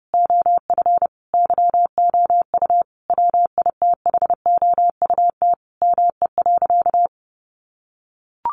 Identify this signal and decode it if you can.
Morse code, heard as a keyed tone, about 30 wpm, 700 Hz. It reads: OF YOU WITHOUT ME.
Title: Sentences from Top 1200 Words 30wpm